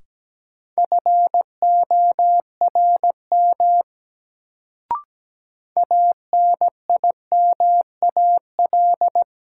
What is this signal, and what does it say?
Morse code, heard as a keyed tone, about 17 wpm, 700 Hz. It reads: FORM ANIMAL
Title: Sentences from Top 1000 Words - Review Part 1 - New Word Drill - 17wpm